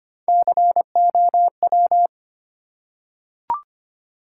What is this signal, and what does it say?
Morse code, heard as a keyed tone, about 25 wpm, 700 Hz. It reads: COW
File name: practice / 25wpm / Easy Words with Hints - 25wpm